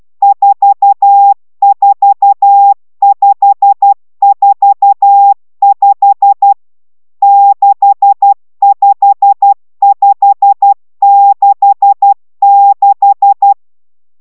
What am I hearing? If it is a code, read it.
Morse code, heard as a keyed tone, about 12 wpm, 800 Hz. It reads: 44545 65566
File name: taviro tanulas / cw 69 20 10